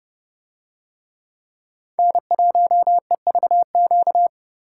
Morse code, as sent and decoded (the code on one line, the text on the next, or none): -. .---- . ...- --.-
N1EVQ